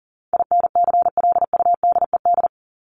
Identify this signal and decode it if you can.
Morse code, heard as a keyed tone, about 40 wpm, 700 Hz. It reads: INCLUDED
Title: ED Suffix - Top 100 Words with Base Words 40wpm